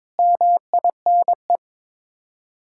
Morse code, heard as a keyed tone, about 22 wpm, 700 Hz. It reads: MINE
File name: practice / 22wpm / Sets of 7 Words - 22wpm